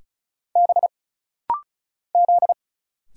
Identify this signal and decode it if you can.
Morse code, heard as a keyed tone, about 34 wpm, 700 Hz. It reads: B Z
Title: Single Letters - Rapid-Fire 35wpm